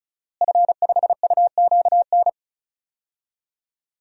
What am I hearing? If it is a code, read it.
Morse code, heard as a keyed tone, about 34 wpm, 700 Hz. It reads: F5UQN
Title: DXCC-100 Call Signs - 5 Characters - Repeated 3x - 35wpm